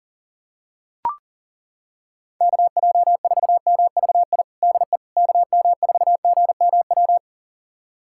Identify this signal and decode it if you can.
Morse code, heard as a keyed tone, about 40 wpm, 700 Hz. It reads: KJ4MVI DE KM4GMW
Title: USA Callsign DE Callsign - 6 Characters - 40wpm